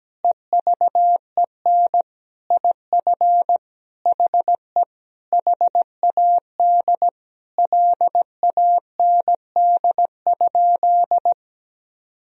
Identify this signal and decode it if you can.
Morse code, heard as a keyed tone, about 17 wpm, 700 Hz. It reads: EVEN IF HE HAD LAND?
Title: Sentences from Top 1000 Words - Review Part 1 - 17wpm